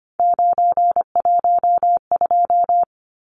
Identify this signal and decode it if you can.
Morse code, heard as a keyed tone, about 25 wpm, 700 Hz. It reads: 912
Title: Numbers - 3-Digit 25wpm